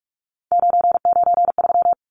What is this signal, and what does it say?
Morse code, heard as a keyed tone, about 44 wpm, 700 Hz. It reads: 993
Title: Numbers - 3-Digit 45wpm